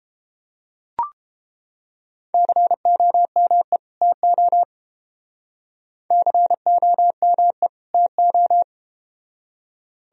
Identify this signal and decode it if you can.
Morse code, heard as a keyed tone, about 33 wpm, 700 Hz. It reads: COME TO COME TO